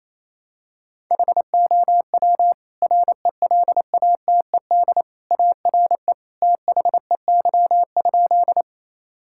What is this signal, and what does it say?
Morse code, heard as a keyed tone, about 28 wpm, 700 Hz. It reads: HOW RELATED ARE THEY?
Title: Sentences from Top 1000 Words - Review Part 5 - 28wpm